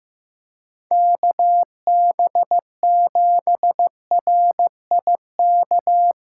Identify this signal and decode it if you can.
Morse code, heard as a keyed tone, about 15 wpm, 700 Hz. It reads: KB7RIK
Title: USA Callsign DE Callsign - 6 Characters - 15wpm